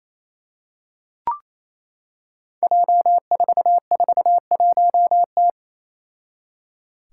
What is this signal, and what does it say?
Morse code, heard as a keyed tone, about 28 wpm, 700 Hz. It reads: J441T
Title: Five Challenging Letter-Number Combinations 28wpm